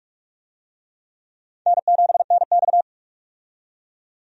Morse code, heard as a keyed tone, about 46 wpm, 700 Hz. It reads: N7NX
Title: USA Call Signs - 4 Characters - 45wpm